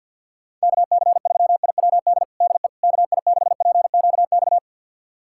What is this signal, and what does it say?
Morse code, heard as a keyed tone, about 50 wpm, 690 Hz. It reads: KC3IWD DE KI6PQX
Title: USA Callsign DE Callsign - 6 Characters - 50wpm